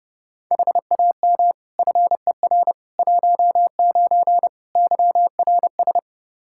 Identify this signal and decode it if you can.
Morse code, heard as a keyed tone, about 30 wpm, 700 Hz. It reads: HAM FER 19 YRS
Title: Practice QSO - Exchange 3 - More Info - 30wpm